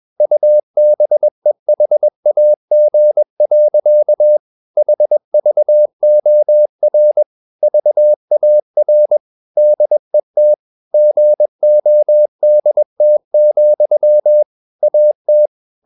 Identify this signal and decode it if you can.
Morse code, heard as a keyed tone, about 21 wpm, 600 Hz. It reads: UBEHAG. HVOR VAR DET GODT, AT
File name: Ferie QRS 21wpm